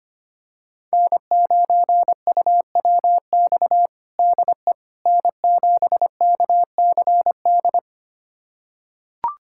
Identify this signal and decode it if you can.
Morse code, heard as a keyed tone, about 25 wpm, 700 Hz. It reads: N9UWX DE N7KCD
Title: USA Callsign DE Callsign - 5 Characters - 25wpm